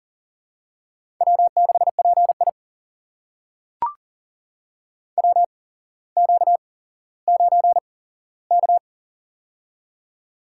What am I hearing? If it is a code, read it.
Morse code, heard as a keyed tone, about 40 wpm, spaced as for 17 wpm, 700 Hz. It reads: W6PI WQ9K
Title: USA Call Signs - 4 Characters - 8x Character Spacing - 40wpm